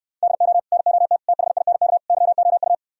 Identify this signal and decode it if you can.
Morse code, heard as a keyed tone, about 67 wpm, 700 Hz. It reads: DO NOT THINK YOU